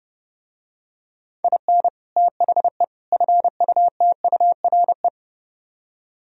Morse code, heard as a keyed tone, about 32 wpm, 700 Hz. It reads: IN THE FUTURE